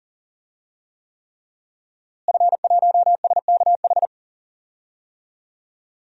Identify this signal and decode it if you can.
Morse code, heard as a keyed tone, about 37 wpm, 700 Hz. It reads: F1SKH
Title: DXCC-100 Call Signs - 5 Characters - Repeated 3x - 40wpm